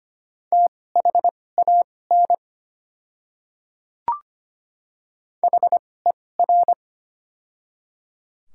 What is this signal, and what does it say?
Morse code, heard as a keyed tone, about 25 wpm, spaced as for 18 wpm, 700 Hz. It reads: THAN HER
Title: 2x Character Spacing - Top 200 Words - 25wpm